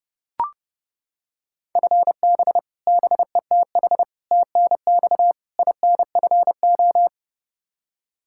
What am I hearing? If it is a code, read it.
Morse code, heard as a keyed tone, about 30 wpm, 700 Hz. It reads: FB BETH TNX INFO
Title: Practice QSO - Exchange 3 - More Info - 30wpm